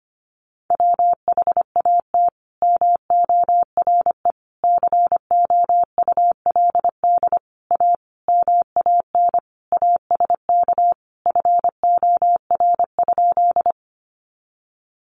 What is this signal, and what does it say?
Morse code, heard as a keyed tone, about 25 wpm, 700 Hz. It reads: WHAT MORE COULD A MAN ASK FOR?
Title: Sentences from Top 200 Words 25wpm